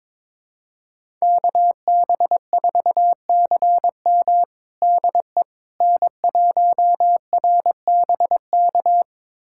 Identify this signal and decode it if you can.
Morse code, heard as a keyed tone, about 22 wpm, 700 Hz. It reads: KB4CM DE N1RBK
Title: USA Callsign DE Callsign - 5 Characters - 22wpm